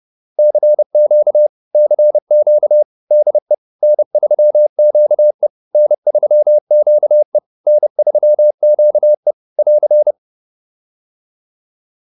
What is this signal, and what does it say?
Morse code, heard as a keyed tone, about 30 wpm, 600 Hz. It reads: CQ CQ DE N3QE N3QE N3QE <AR>